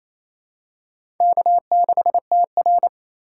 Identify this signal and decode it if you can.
Morse code, heard as a keyed tone, about 27 wpm, 700 Hz. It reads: K6TR
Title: USA Call Signs - 4 Characters - 6x Character Spacing - 28wpm